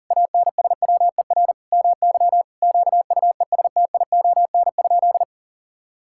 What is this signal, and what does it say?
Morse code, heard as a keyed tone, about 37 wpm, 700 Hz. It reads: ANSWER MY QUESTION?
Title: Sentences from Top 1000 Words - Review Part 5 - 40wpm